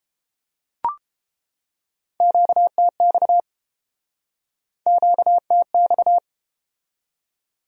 Morse code, heard as a keyed tone, about 30 wpm, 700 Hz. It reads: QTX QTX